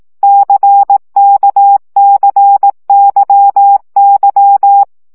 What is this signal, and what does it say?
Morse code, heard as a keyed tone, about 18 wpm, 800 Hz. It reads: CKCYY
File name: CW gyakorlas k c y 15 baud